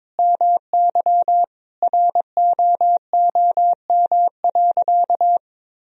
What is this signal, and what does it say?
Morse code, heard as a keyed tone, about 22 wpm, 700 Hz. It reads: MY ROOM.
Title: Sentences from Top 1000 Words - Review Part 1 - 22wpm